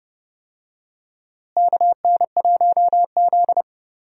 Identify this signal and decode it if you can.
Morse code, heard as a keyed tone, about 32 wpm, 700 Hz. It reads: KN1Z
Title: USA Call Signs - 4 Characters - Repeated 3x - 30wpm